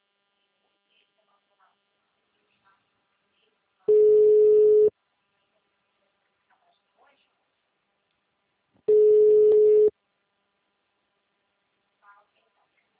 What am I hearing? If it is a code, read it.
Ringback tone (Europe, 425 Hz).
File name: out- -2006- - - .37214